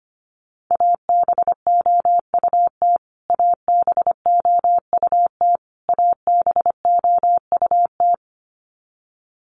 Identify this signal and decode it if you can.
Morse code, heard as a keyed tone, about 25 wpm, 700 Hz. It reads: ABOUT ABOUT ABOUT
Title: Top 200 Words Repeated 25wpm